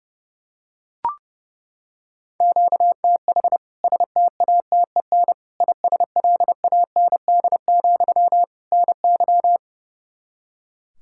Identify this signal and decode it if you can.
Morse code, heard as a keyed tone, about 30 wpm, 700 Hz. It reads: QTH STATEN ISLAND, NY
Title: K1USN SST QSO Practice 30wpm